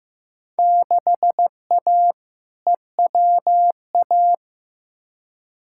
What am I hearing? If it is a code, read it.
Morse code, heard as a keyed tone, about 16 wpm, 700 Hz. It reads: 6A EWA
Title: Field Day Exchanges Encore - 15wpm